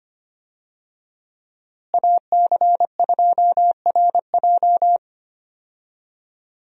Morse code, heard as a keyed tone, about 25 wpm, 700 Hz. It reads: AC2RJ
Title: USA Call Signs - 5 Characters - Repeated 3x - 25wpm